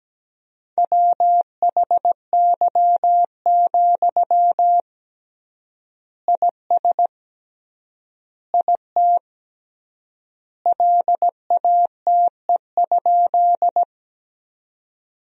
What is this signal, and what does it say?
Morse code, heard as a keyed tone, about 17 wpm, 700 Hz. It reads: WHY, IS IT LATE?
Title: 3x Word-Spacing - Sentences from Top 300 Words - 17wpm